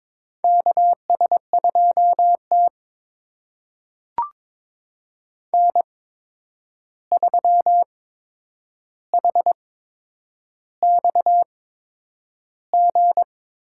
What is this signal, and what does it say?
Morse code, heard as a keyed tone, about 22 wpm, spaced as for 9 wpm, 700 Hz. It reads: KS2T N3HXG
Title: USA Call Signs - 8x Character Spacing - 22wpm